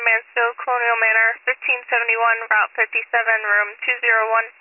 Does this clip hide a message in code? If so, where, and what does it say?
Yes: Emergency Alert System (SAME) tones from 1.0 to 4.2 s, heard as harsh buzzing bursts, with speech.